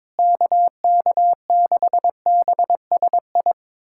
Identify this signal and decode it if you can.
Morse code, heard as a keyed tone, about 22 wpm, 700 Hz. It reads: KK6BSI
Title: USA Callsign DE Callsign - 6 Characters - 22wpm